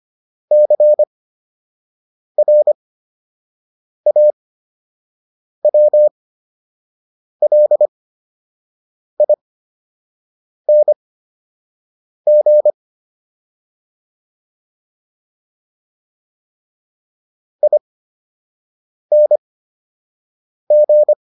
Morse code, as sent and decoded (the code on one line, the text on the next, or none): -.-. .-. .- .-- .-.. .. -. --. / .. -. --.
CRAWLING ING